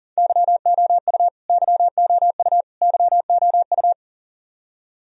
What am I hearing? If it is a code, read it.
Morse code, heard as a keyed tone, about 40 wpm, 700 Hz. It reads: YOU YOU YOU
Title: Top 100 Words Repeated 40wpm